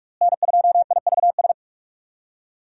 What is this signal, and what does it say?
Morse code, heard as a keyed tone, about 46 wpm, 700 Hz. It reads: N2IVS
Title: USA Call Signs - 5 Characters - Repeated 2x - 45wpm